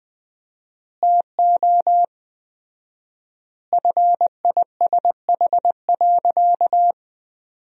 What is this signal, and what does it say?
Morse code, heard as a keyed tone, about 21 wpm, 700 Hz. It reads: TO FISH.
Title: 4x Word-Spacing - Sentences from Top 300 Words - 20wpm